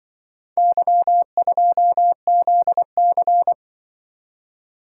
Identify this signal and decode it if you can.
Morse code, heard as a keyed tone, about 24 wpm, 700 Hz. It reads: Y2ZC
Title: Speed-Racing - Four Letter-Number Combinations - 20wpm